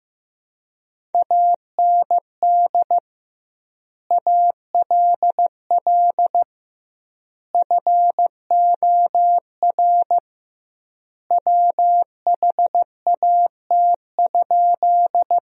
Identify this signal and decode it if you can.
Morse code, heard as a keyed tone, about 15 wpm, 700 Hz. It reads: AND ALL FOR WHAT?
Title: 2x Word-Spacing - Sentences from Top 100 Words - 15wpm